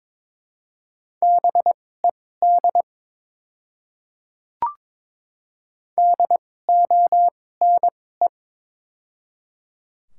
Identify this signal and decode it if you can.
Morse code, heard as a keyed tone, about 21 wpm, spaced as for 16 wpm, 700 Hz. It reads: BED DONE
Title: 2x Character Spacing - Top 500 Words - 22wpm